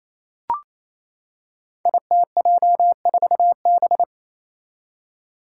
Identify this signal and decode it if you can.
Morse code, heard as a keyed tone, about 29 wpm, 700 Hz. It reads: ITJ4B
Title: Five Challenging Letter-Number Combinations 28wpm